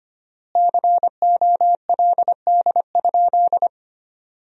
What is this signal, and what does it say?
Morse code, heard as a keyed tone, about 25 wpm, 700 Hz. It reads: COLD?
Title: 4x Word-Spacing - Sentences from Top 400 Words - 25wpm